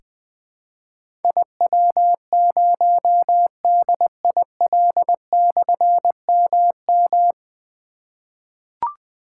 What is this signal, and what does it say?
Morse code, heard as a keyed tone, about 20 wpm, 700 Hz. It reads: IW0DIL/MM
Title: Difficult Call Signs 20wpm